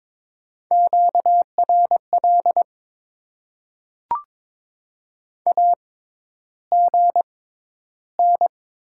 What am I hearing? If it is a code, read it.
Morse code, heard as a keyed tone, about 22 wpm, spaced as for 11 wpm, 700 Hz. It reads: QRL AGN
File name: Top 20 QSO Elements - 6x Character Spacing - 22wpm